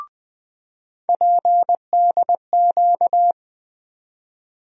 Morse code, heard as a keyed tone, about 20 wpm, 700 Hz. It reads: PDQ
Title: USA Call Sign Postfixes - 3 Letters 20wpm